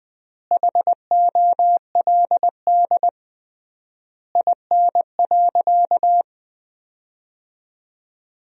Morse code, heard as a keyed tone, about 21 wpm, 700 Hz. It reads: HOLD IN.
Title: 3x Word-Spacing - Sentences from Top 400 Words - 20wpm